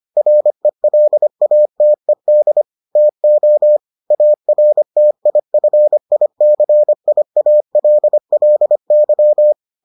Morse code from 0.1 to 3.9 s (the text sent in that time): RELATED TO